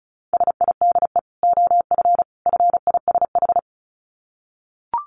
Morse code, heard as a keyed tone, about 34 wpm, 700 Hz. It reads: SIDE OF FISH
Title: Dit Challenge 35wpm